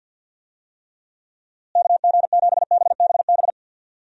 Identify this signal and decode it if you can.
Morse code, heard as a keyed tone, about 51 wpm, 700 Hz. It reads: KG7BBB
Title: USA Call Signs - 6 Characters - Repeated 2x - 50wpm